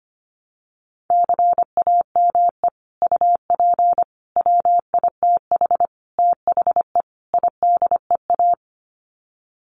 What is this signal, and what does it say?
Morse code, heard as a keyed tone, about 25 wpm, 700 Hz. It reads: CAME UP WITH THE IDEA
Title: Sets of 5 Words Encore2 25wpm